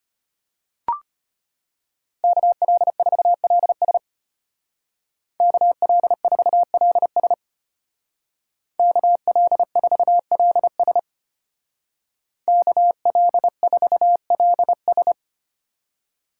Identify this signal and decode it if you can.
Morse code, heard as a keyed tone, about 30 wpm, 700 Hz. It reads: KL4LS KL4LS KL4LS KL4LS
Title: Speed Racing - USA 2x2 Call Signs 25wpm